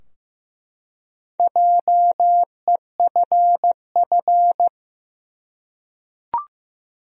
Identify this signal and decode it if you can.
Morse code, heard as a keyed tone, about 15 wpm, 700 Hz. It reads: JEFF
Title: USA Common Names - Max 4 Letters - 15wpm